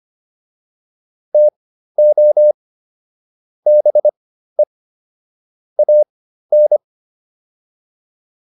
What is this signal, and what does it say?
Morse code, heard as a keyed tone, about 25 wpm, spaced as for 13 wpm, 600 Hz. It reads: TO BE AN